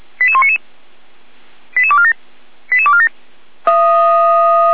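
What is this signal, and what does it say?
Five-tone selective call (ZVEI), heard as quick runs of five tones, tones of about 70 ms. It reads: 88199 88247 88247
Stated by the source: ZVEI five-tone selective call for address 88191, received by SDR, from the feuerwehr-orlamuende (Orlamünde fire brigade) audio server.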